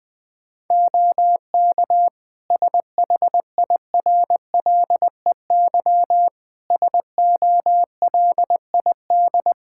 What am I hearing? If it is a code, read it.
Morse code, heard as a keyed tone, about 20 wpm, 700 Hz. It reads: OK SHIRLEY SOLID
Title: Practice QSO - Exchange 3 - More Info - 20wpm